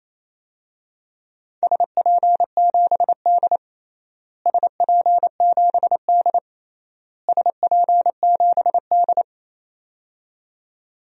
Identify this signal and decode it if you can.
Morse code, heard as a keyed tone, about 28 wpm, 700 Hz. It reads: SP7D SP7D SP7D
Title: DXCC-100 Call Signs - 4 Characters - Repeated 3x - 28wpm